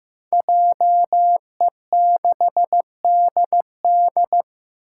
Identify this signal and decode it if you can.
Morse code, heard as a keyed tone, about 15 wpm, 700 Hz. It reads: JE6DD
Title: Five Challenging Letter-Number Combinations 15wpm